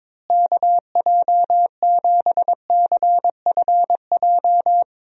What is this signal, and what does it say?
Morse code, heard as a keyed tone, about 22 wpm, 700 Hz. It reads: KJ7CFJ
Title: USA Call Signs - 6 Characters - Repeated 2x - 22wpm